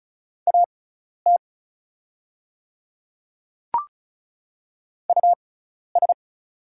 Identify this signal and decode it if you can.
Morse code, heard as a keyed tone, about 36 wpm, spaced as for 12 wpm, 690 Hz. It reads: AT US
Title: 6x Character Spacing - Top 300 Words - 35wpm